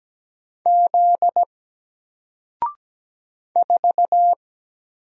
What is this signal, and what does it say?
Morse code, heard as a keyed tone, about 17 wpm, 700 Hz. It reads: Z 4
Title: Single Character (Letter:Number:Punctuation:Prosign) 17wpm